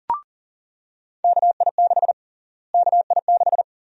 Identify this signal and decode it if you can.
Morse code, heard as a keyed tone, about 37 wpm, 700 Hz. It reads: KI6 KI6
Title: USA Call Sign Prefixes - Repeated 2x - 40wpm